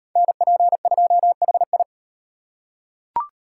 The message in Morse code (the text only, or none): NP2HI